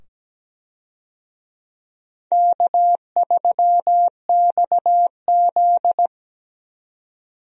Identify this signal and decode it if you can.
Morse code, heard as a keyed tone, about 17 wpm, 700 Hz. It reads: K3XZ
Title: USA Call Signs - 4 Characters - 17wpm